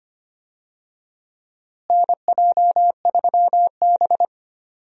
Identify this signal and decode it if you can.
Morse code, heard as a keyed tone, about 25 wpm, 700 Hz. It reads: NJ3B
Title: USA Call Signs - 4 Characters - Repeated 3x - 25wpm